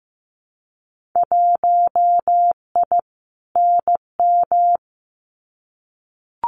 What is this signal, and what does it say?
Morse code, heard as a keyed tone, about 15 wpm, 700 Hz. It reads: 1I NM
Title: Winter Field Day Exchanges Encore 15wpm